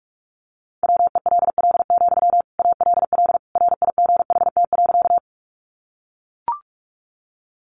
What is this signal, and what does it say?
Morse code, heard as a keyed tone, about 44 wpm, 700 Hz. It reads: WELL, ALL RIGHT.
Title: Sentences from Top 200 Words 45wpm